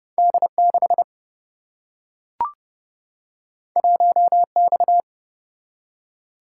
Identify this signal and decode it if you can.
Morse code, heard as a keyed tone, about 30 wpm, 700 Hz. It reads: D 6 1 X
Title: Single-Letter Single-Number 30wpm